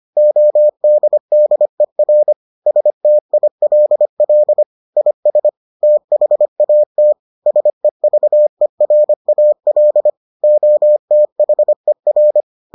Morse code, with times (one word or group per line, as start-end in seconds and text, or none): 0.2-2.3 ODDER
2.7-4.6 STILL
5.0-5.5 IS
5.8-7.1 THAT
7.5-10.1 SEVERAL
10.4-12.4 OTHER